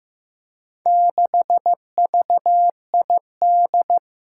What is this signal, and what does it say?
Morse code, heard as a keyed tone, about 15 wpm, 700 Hz. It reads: 6VID
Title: Four Challenging Letter-Number Combinations 15wpm